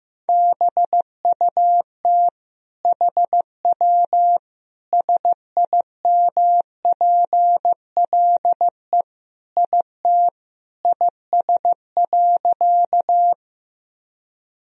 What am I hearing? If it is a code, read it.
Morse code, heard as a keyed tone, about 15 wpm, 700 Hz. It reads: BUT HW SIMPLE IT IS.
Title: Sentences from Top 400 Words and Abbreviations 15wpm